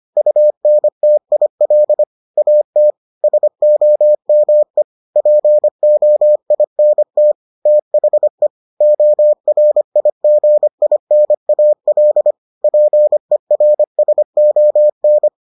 Morse code, heard as a keyed tone, about 25 wpm, 600 Hz. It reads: UNTIL AT SOME POINT THE ORIGINAL PERSON